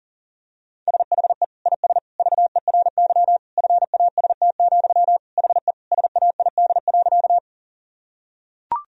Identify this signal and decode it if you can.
Morse code, heard as a keyed tone, about 40 wpm, 700 Hz. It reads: SHE IS VERY FAST, HE SAID.
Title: Sentences from Top 1000 Words - Review Part 2 - 40wpm